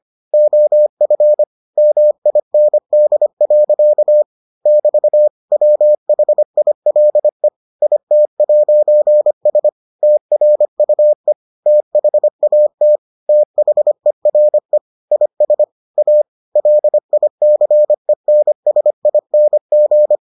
Morse code, heard as a keyed tone, about 25 wpm, 600 Hz. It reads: OF MIND. <BT> WHILE IT'S TRUE THAT THERE IS A LICENSING